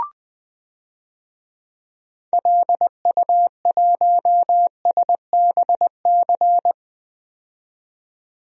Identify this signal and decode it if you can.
Morse code, heard as a keyed tone, about 20 wpm, 700 Hz. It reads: LU1SBC
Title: DXCC-100 Call Signs - 6 Characters - 20wpm